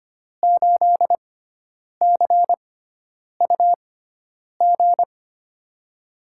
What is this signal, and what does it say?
Morse code, heard as a keyed tone, about 25 wpm, spaced as for 9 wpm, 700 Hz. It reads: 8CUG
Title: USA Call Signs - 5 Characters - 6x Character Spacing - 25wpm